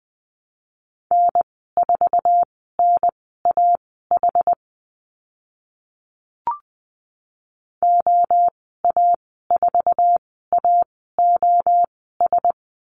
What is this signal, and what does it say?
Morse code, heard as a keyed tone, about 20 wpm, spaced as for 14 wpm, 700 Hz. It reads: N4NAH OA4AOS
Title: 069 Lesson 12g - Letter d - Call Signs 20wpm